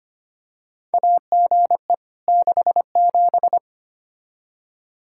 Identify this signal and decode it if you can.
Morse code, heard as a keyed tone, about 25 wpm, 700 Hz. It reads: AGE 67
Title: K1USN SST QSO Practice 25wpm